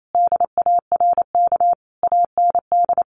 Morse code, heard as a keyed tone, about 28 wpm, 700 Hz. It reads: DARK AND
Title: Sentences from Top 1400 Words 28wpm